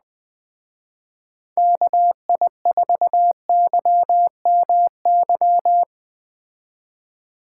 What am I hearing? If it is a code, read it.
Morse code, heard as a keyed tone, about 20 wpm, 700 Hz. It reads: KI4YMY